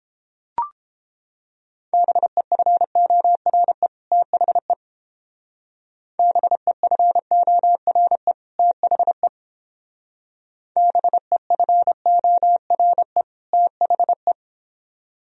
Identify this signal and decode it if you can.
Morse code, heard as a keyed tone, about 31 wpm, 700 Hz. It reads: BEFORE THE BEFORE THE BEFORE THE